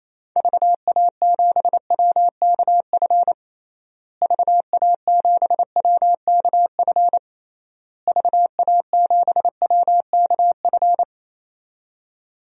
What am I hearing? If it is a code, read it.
Morse code, heard as a keyed tone, about 28 wpm, 700 Hz. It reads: VA7WKF VA7WKF VA7WKF
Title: DXCC-100 Call Signs - 6 Characters - Repeated 3x - 28wpm